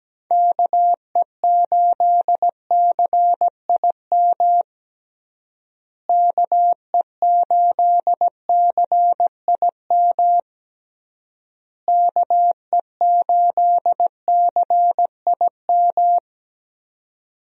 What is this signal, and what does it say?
Morse code, heard as a keyed tone, about 17 wpm, 700 Hz. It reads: KE8CIM KE8CIM KE8CIM
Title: USA Call Signs - 6 Characters - Repeated 3x - 17wpm